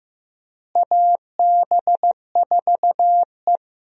Morse code, heard as a keyed tone, about 15 wpm, 700 Hz. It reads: AB4E
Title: USA Call Signs - 4 Characters - Repeated 3x - 15wpm